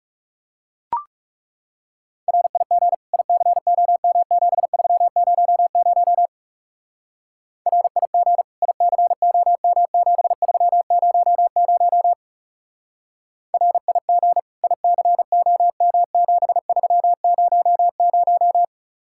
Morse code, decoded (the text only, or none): RIG ICOM7300 RIG ICOM7300 RIG ICOM7300